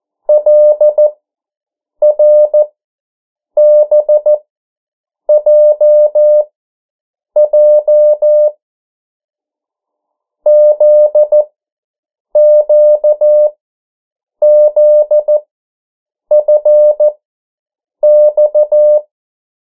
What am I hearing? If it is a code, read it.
Morse code, heard as a keyed tone, about 14 wpm, spaced as for 7 wpm, 600 Hz. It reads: LRBJJ ZQZFX
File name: morse mp3 / 09 urok 8